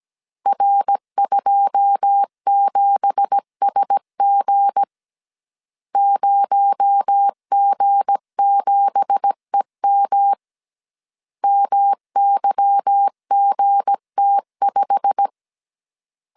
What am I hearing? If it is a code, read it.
Morse code, heard as a keyed tone, about 17 wpm, 800 Hz. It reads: R27SG 0G7EM MYGT5
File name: CW / DET / 1 9 1500C S A 16wpm 2828